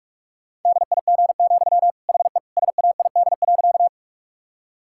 Morse code, heard as a keyed tone, about 46 wpm, 700 Hz. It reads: DIG, HE SAID.
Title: Sentences from Top 1600 Words 45wpm